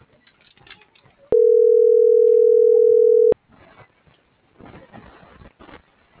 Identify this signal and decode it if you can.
Ringback tone (North America), 440 + 480 Hz.